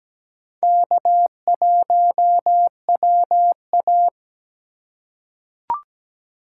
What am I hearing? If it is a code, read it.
Morse code, heard as a keyed tone, about 17 wpm, 700 Hz. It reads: K1WA
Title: USA Call Signs - 4 Characters - 4x Character Spacing - 17wpm